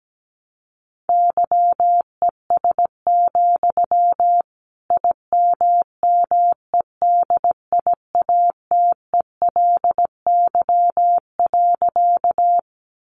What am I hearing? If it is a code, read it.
Morse code, heard as a keyed tone, about 17 wpm, 700 Hz. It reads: YES, IMMEDIATELY.
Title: Sentences from Top 1100 Words 17wpm